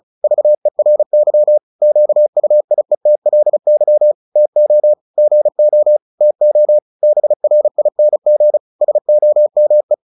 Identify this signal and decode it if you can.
Morse code, heard as a keyed tone, about 36 wpm, 600 Hz. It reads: VERY QUIETLY TO GO TO BRING SOME
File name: the old man and the sea 2 35wpm